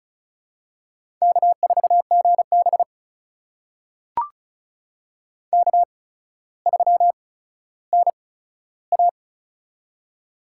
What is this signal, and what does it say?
Morse code, heard as a keyed tone, about 36 wpm, spaced as for 15 wpm, 700 Hz. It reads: K4GB K3NA
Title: USA Call Signs - 8x Character Spacing - 35wpm